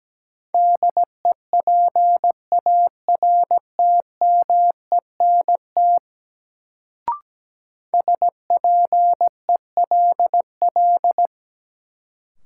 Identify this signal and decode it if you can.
Morse code, heard as a keyed tone, about 17 wpm, 700 Hz. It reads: DEPARTMENT SPELL
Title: Sentences from Top 1200 Words - New Word Drill Alt - 17wpm